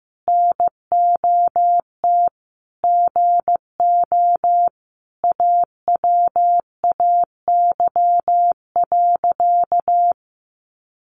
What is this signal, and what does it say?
Morse code, heard as a keyed tone, about 15 wpm, 700 Hz. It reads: NOT GO AWAY.
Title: Sentences from Top 600 Words 15wpm